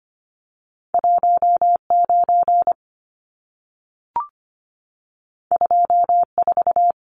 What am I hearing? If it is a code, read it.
Morse code, heard as a keyed tone, about 25 wpm, 700 Hz. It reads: 19 24
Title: Numbers - 2-Digit 25wpm